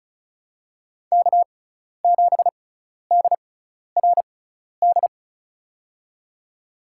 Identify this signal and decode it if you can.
Morse code, heard as a keyed tone, about 36 wpm, spaced as for 12 wpm, 700 Hz. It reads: K7DRD
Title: USA Call Signs - 6x Character Spacing - 35wpm